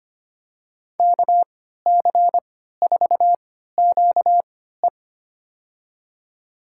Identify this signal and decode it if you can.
Morse code, heard as a keyed tone, about 25 wpm, spaced as for 14 wpm, 700 Hz. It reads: KC4QE
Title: USA Call Signs - 5 Characters - 3x Character Spacing - 25wpm